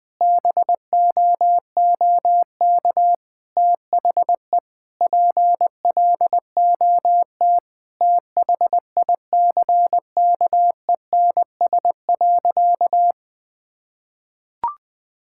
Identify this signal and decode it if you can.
Morse code, heard as a keyed tone, about 20 wpm, 700 Hz. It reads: BOOK THE PLOT THICKENS.
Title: Sentences with 5-Word Phrases - 20wpm